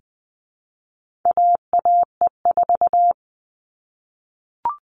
Morse code, heard as a keyed tone, about 20 wpm, 700 Hz. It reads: AAE4
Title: All Cut Numbers - 4-Digit 20wpm